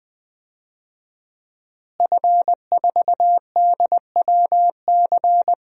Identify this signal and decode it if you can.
Morse code, heard as a keyed tone, about 20 wpm, 700 Hz. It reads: F4DWC
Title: DXCC-100 Call Signs - 5 Characters - Repeated 2x - 20wpm